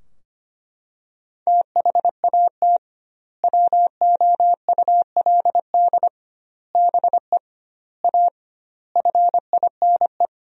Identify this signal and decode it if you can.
Morse code, heard as a keyed tone, about 25 wpm, 700 Hz. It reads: THAT WOULD BE A FINE
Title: 2x Word-Spacing - Sentences from Top 500 Words - 25wpm